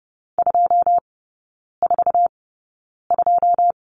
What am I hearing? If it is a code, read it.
Morse code, heard as a keyed tone, about 30 wpm, spaced as for 9 wpm, 700 Hz. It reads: 242
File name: 081 Lesson 14e - Number 2 - Three Characters 30wpm